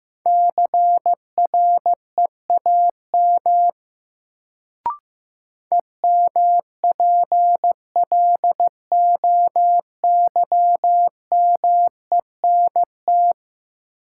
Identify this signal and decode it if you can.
Morse code, heard as a keyed tone, about 15 wpm, 700 Hz. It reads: CREAM EMPLOYMENT
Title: Sentences from Top 1500 Words - New Word Drill Alt - 15wpm